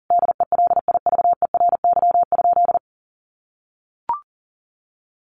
Morse code, heard as a keyed tone, about 40 wpm, 700 Hz. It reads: DELIVERY?
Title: Sentences from Top 1500 Words 40wpm